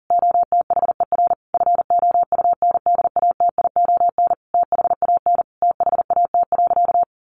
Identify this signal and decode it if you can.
Morse code, heard as a keyed tone, about 40 wpm, 700 Hz. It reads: OTHER FOUNDATION THAN THAT.
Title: Sentences from Top 1200 Words 40wpm